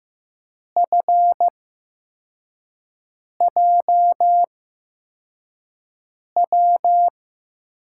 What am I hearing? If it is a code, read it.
Morse code, heard as a keyed tone, about 15 wpm, spaced as for 4 wpm, 700 Hz. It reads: FJW
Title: USA Call Signs - 6 Characters - 8x Character Spacing - 15wpm